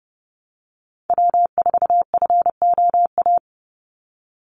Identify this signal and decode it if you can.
Morse code, heard as a keyed tone, about 30 wpm, 700 Hz. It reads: W4FOA
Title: Call Signs - 30wpm